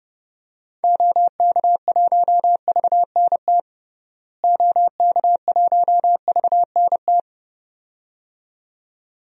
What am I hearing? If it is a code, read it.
Morse code, heard as a keyed tone, about 30 wpm, 700 Hz. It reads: OK1VNT OK1VNT
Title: DXCC-100 Call Signs - 6 Characters - Repeated 2x - 30wpm